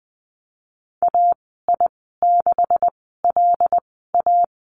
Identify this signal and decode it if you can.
Morse code, heard as a keyed tone, about 20 wpm, spaced as for 15 wpm, 700 Hz. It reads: AI6LA
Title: 167 Lesson 27g - Number 7 - Call Signs 20wpm